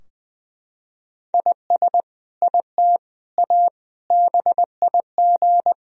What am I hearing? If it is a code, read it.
Morse code, heard as a keyed tone, about 20 wpm, 700 Hz. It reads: IS IT A BIG
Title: Sentences from Top 1000 Words - Review Part 4 - 20wpm